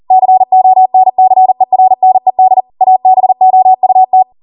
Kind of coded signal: Morse code